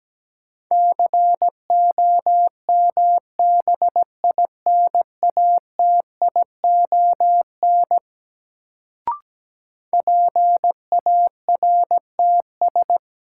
Morse code, heard as a keyed tone, about 17 wpm, 700 Hz. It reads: COMBINATION PARTS
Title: Sentences from Top 1000 Words - Review Part 6 - New Word Drill - 17wpm